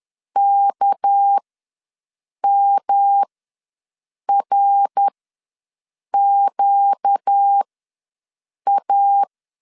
Morse code, heard as a keyed tone, about 10 wpm, spaced as for 6 wpm, 800 Hz. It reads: KMRQA
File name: LEZIONI / 6 PKX0 / 6 PKX ws 10wpm 3706